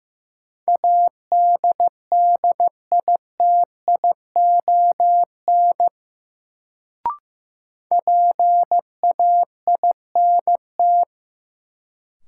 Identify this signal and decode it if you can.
Morse code, heard as a keyed tone, about 15 wpm, 700 Hz. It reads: ADDITION PAINT
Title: Sentences from Top 1000 Words - Review Part 6 - New Word Drill - 15wpm